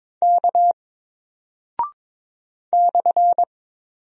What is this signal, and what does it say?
Morse code, heard as a keyed tone, about 22 wpm, 700 Hz. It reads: K /
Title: Single Character (Letter:Number:Punctuation:Prosign) Alt 22wpm